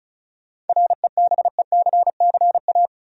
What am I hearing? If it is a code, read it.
Morse code, heard as a keyed tone, about 36 wpm, 700 Hz. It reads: REBECCA
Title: USA Common Names - 35wpm